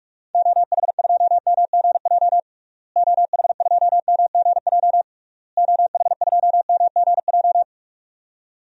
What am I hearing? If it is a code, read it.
Morse code, heard as a keyed tone, about 41 wpm, 700 Hz. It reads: OH2MGJ OH2MGJ OH2MGJ
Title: DXCC-100 Call Signs - 6 Characters - Repeated 3x - 45wpm